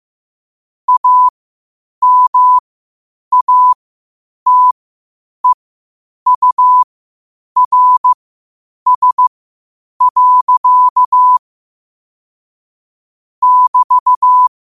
Morse code, heard as a keyed tone, about 15 wpm, spaced as for 8 wpm, 1000 Hz. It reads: AMATEURS. <BT>
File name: qotdfiles / 8wpm